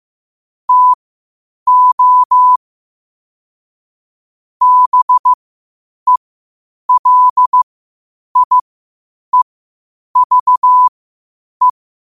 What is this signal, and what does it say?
Morse code, heard as a keyed tone, about 15 wpm, spaced as for 8 wpm, 1000 Hz. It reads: TO BELIEVE
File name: qotdfiles / 8wpm